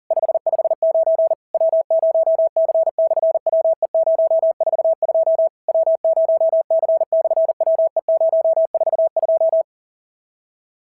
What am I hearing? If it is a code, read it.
Morse code, heard as a keyed tone, about 40 wpm, 690 Hz. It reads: 559 W0C/WE042 W0C/WE042
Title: SOTA QSO Practice 40wpm